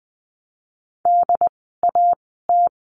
Morse code, heard as a keyed tone, about 20 wpm, spaced as for 15 wpm, 700 Hz. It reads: DAT
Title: 075 Lesson 13f - Letter L - Words 20wpm